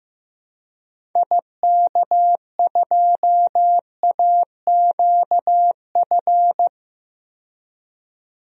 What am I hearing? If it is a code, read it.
Morse code, heard as a keyed tone, about 15 wpm, 700 Hz. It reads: IK2AQF
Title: DXCC-100 Call Signs - 6 Characters - 15wpm